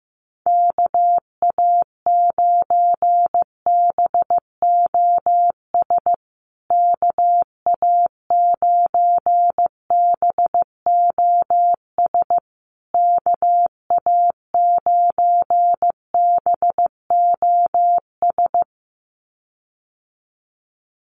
Morse code, heard as a keyed tone, about 15 wpm, 700 Hz. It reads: KA9BOS KA9BOS KA9BOS
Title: USA Call Signs Repeated 3x 15wpm